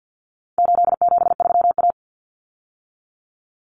Morse code, heard as a keyed tone, about 50 wpm, 700 Hz. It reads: 873A
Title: Common Cut Numbers - 4-Digit 50wpm